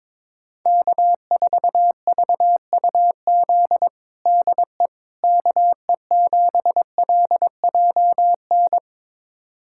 Morse code, heard as a keyed tone, about 22 wpm, 700 Hz. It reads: K4VUZ DE KE7LJN
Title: USA Callsign DE Callsign 22wpm